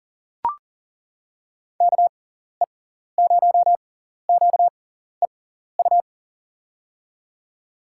Morse code, heard as a keyed tone, about 38 wpm, spaced as for 14 wpm, 700 Hz. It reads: KE0QEU
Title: USA Call Signs - 6 Characters - 6x Character Spacing - 40wpm